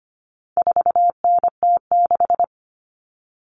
Morse code, heard as a keyed tone, about 25 wpm, 700 Hz. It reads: 4NT6
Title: All Cut Numbers - 4-Digit 25wpm